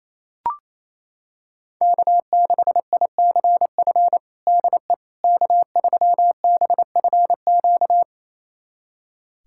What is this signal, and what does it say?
Morse code, heard as a keyed tone, about 28 wpm, 700 Hz. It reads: K6ICF DE K3BFQ
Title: USA Callsign DE Callsign - 5 Characters - 28wpm